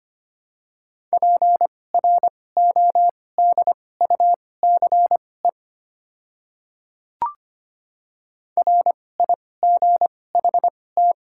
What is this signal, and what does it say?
Morse code, heard as a keyed tone, about 25 wpm, spaced as for 18 wpm, 700 Hz. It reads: PRODUCE RIGHT
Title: 2x Character Spacing - Top 500 Words - 25wpm